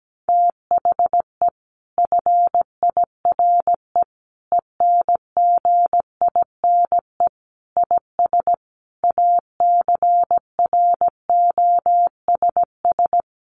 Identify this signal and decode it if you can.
Morse code, heard as a keyed tone, about 17 wpm, 700 Hz. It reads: THE FIRE ENGINE IS ACROSS
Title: Sentences from Top 1500 Words 17wpm